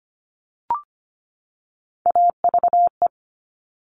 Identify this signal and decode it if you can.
Morse code, heard as a keyed tone, about 25 wpm, 700 Hz. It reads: AVE
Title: All Cut Numbers - 3-Digit 25wpm